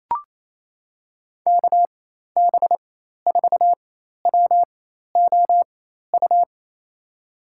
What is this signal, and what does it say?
Morse code, heard as a keyed tone, about 27 wpm, spaced as for 13 wpm, 700 Hz. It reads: KB4WOU
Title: USA Call Signs - 6 Characters - 4x Character Spacing - 28wpm